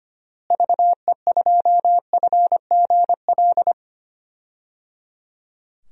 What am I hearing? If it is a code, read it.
Morse code, heard as a keyed tone, about 25 wpm, 700 Hz. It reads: VE2FGL
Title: DXCC-100 Call Signs - 6 Characters - Repeated 3x - 25wpm